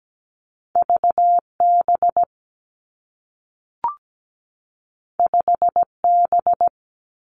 Morse code, heard as a keyed tone, about 17 wpm, 700 Hz. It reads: VB 5B
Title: Two Challenging Letter-Number Combinations 17wpm